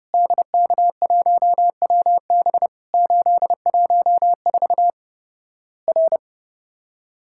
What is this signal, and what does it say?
Morse code, heard as a keyed tone, about 30 wpm, 700 Hz. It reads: DK1WB 814 R
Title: Worked All Europe DX Contest - QTC Exchange - 30wpm